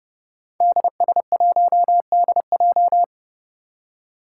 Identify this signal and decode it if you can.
Morse code, heard as a keyed tone, about 30 wpm, 700 Hz. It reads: DS1DJ
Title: Five Challenging Letter-Number Combinations 30wpm